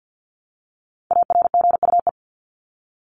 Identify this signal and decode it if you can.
Morse code, heard as a keyed tone, about 50 wpm, 700 Hz. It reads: ARGUE
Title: Top 1501-1750 Words 50wpm